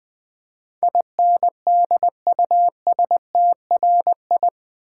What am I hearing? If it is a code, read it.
Morse code, heard as a keyed tone, about 20 wpm, 700 Hz. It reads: INDUSTRI
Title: Build A Word - 4 Letters At A Time - 10-Letter Words - 20wpm